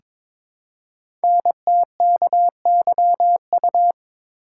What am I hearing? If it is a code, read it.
Morse code, heard as a keyed tone, about 22 wpm, 700 Hz. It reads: NTKYU
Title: Code Groups - 5 Letters - 22wpm